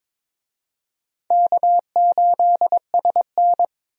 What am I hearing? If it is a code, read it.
Morse code, heard as a keyed tone, about 22 wpm, 700 Hz. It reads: K8SN
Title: USA Call Signs - 4 Characters - Repeated 2x - 22wpm